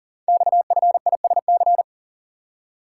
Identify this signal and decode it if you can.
Morse code, heard as a keyed tone, about 42 wpm, 700 Hz. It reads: XFISC
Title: Code Groups - 5 Letters - 40wpm